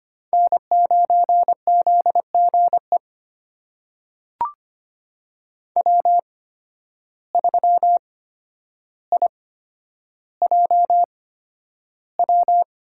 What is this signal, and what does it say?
Morse code, heard as a keyed tone, about 25 wpm, spaced as for 11 wpm, 700 Hz. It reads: N9ZGE W3IJW
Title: USA Call Signs - 5 Characters - 8x Character Spacing - 25wpm